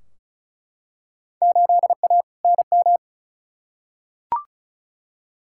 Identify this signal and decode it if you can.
Morse code, heard as a keyed tone, about 36 wpm, 700 Hz. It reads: 8A NM
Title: Field Day Exchanges Encore - 35wpm